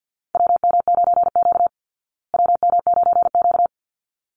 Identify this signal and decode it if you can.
Morse code, heard as a keyed tone, about 50 wpm, 700 Hz. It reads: WM9Q WM9Q
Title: Call Signs Repeated 3x - 50wpm